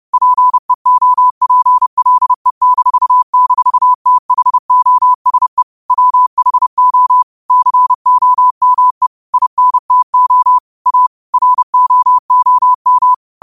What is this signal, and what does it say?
Morse code, heard as a keyed tone, about 30 wpm, 1000 Hz. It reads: PEOPLE--THOSE WHO COME INTO A ROOM
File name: qotdfiles / 30wpm